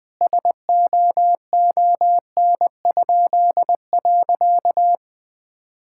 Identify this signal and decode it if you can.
Morse code, heard as a keyed tone, about 20 wpm, 700 Hz. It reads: SOON?.
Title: Pangram Challenge v2 - 20wpm